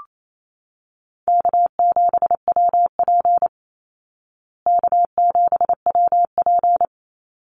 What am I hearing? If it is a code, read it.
Morse code, heard as a keyed tone, about 28 wpm, 700 Hz. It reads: K7WP K7WP
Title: Easy Call Signs Repeated 3x - 28wpm